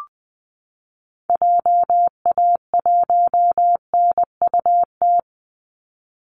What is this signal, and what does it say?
Morse code, heard as a keyed tone, about 20 wpm, 700 Hz. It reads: JA1NUT
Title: Call Signs Repeated 2x - 20wpm